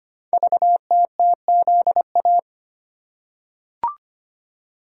Morse code, heard as a keyed tone, about 25 wpm, 700 Hz. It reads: VTTZA
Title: Code Groups - 5 Letters - 25wpm